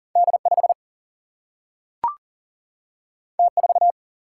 Morse code, heard as a keyed tone, about 40 wpm, 700 Hz. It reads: D5 T4
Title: Single-Letter Single-Number 40wpm